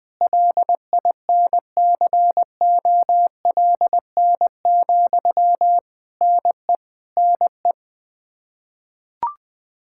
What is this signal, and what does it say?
Morse code, heard as a keyed tone, about 20 wpm, 700 Hz. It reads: LINCOLN, NE NE
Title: Practice QSO - Exchange 1 - Beginning - 20wpm